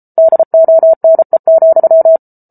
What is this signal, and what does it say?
Morse code, heard as a keyed tone, about 34 wpm, 650 Hz. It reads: DONE,